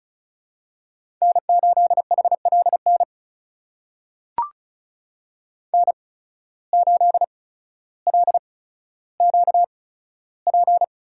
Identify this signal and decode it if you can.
Morse code, heard as a keyed tone, about 36 wpm, spaced as for 15 wpm, 700 Hz. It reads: N8HLN N8LQP
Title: USA Call Signs - 5 Characters - 8x Character Spacing - 35wpm